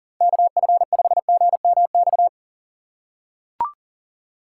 Morse code, heard as a keyed tone, about 40 wpm, 700 Hz. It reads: KF5GMX
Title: USA Call Signs - 6 Characters - 4x Character Spacing - 40wpm